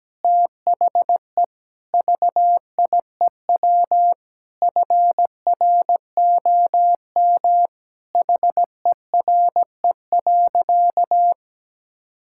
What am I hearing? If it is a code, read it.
Morse code, heard as a keyed tone, about 17 wpm, 700 Hz. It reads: THE VIEW FROM HERE.
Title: Sentences from Top 1000 Words - Review Part 6 - 17wpm